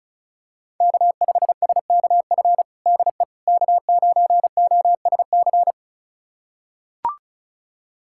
Morse code, heard as a keyed tone, about 36 wpm, 700 Hz. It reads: K5SKF DE K9OSC
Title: USA Callsign DE Callsign - 5 Characters - 35wpm